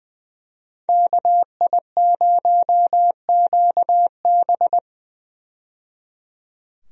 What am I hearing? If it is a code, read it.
Morse code, heard as a keyed tone, about 20 wpm, 700 Hz. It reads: KI0QB
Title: USA Call Signs - 5 Characters - 20wpm